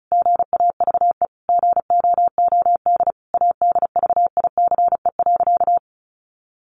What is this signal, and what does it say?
Morse code, heard as a keyed tone, about 34 wpm, 700 Hz. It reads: GAVE GOOD ADVICE.
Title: Sentences from Top 1100 Words 35wpm